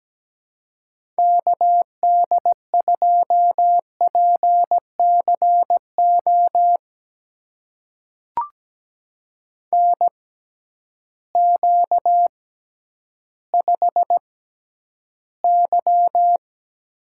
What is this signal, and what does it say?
Morse code, heard as a keyed tone, about 17 wpm, spaced as for 10 wpm, 700 Hz. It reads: KD2PCO NQ5Y
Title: USA Call Signs - 6x Character Spacing - 17wpm